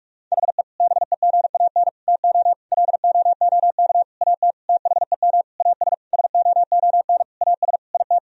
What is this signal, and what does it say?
Morse code, heard as a keyed tone, about 41 wpm, 700 Hz. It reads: HE BEGAN TO LOOK AT THEM AS SOON AS IT
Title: Sentences from Top 1000 Words - Review Part 1 - 45wpm